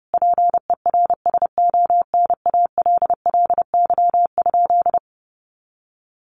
Morse code, heard as a keyed tone, about 30 wpm, 700 Hz. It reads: PERSONALLY?
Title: Sentences from Top 1400 Words 30wpm